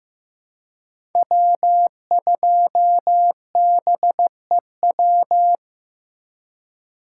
Morse code, heard as a keyed tone, about 15 wpm, 700 Hz. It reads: W2BEW
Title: USA Call Signs - 5 Characters - 3x Character Spacing - 15wpm